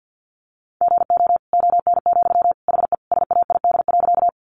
Morse code, heard as a keyed tone, about 50 wpm, 700 Hz. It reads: GO ON, HE SAID.
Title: Sentences from Top 100 Words 50wpm